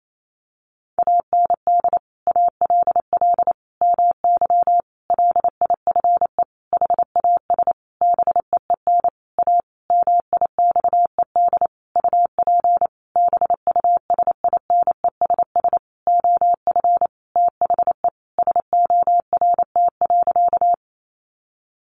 Morse code, heard as a keyed tone, about 28 wpm, 700 Hz. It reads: AND ALL MY LIFE HAS BEEN A MIXED UP BUSINESS OF THE SORT.
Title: Sentences from Top 1500 Words 28wpm